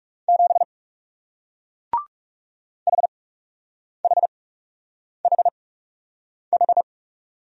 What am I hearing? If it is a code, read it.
Morse code, heard as a keyed tone, about 39 wpm, spaced as for 8 wpm, 700 Hz. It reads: 7 HHHH